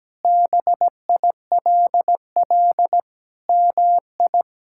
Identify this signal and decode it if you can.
Morse code, heard as a keyed tone, about 17 wpm, 700 Hz. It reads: BILL MI
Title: K1USN SST QSO Practice - 17wpm